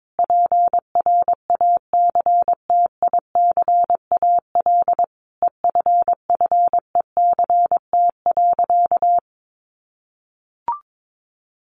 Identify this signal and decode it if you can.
Morse code, heard as a keyed tone, about 22 wpm, 700 Hz. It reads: PRACTICAL EFFECT.
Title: Sentences from Top 1400 Words 22wpm